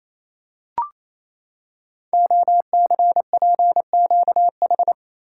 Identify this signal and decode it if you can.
Morse code, heard as a keyed tone, about 29 wpm, 700 Hz. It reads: OCPQH
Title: Code Groups - 5 Letter:Numbers - 28wpm